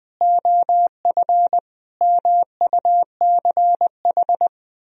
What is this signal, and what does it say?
Morse code, heard as a keyed tone, about 20 wpm, 700 Hz. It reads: OF MUCH
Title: Sentences from Top 1000 Words - Review Part 3 - 20wpm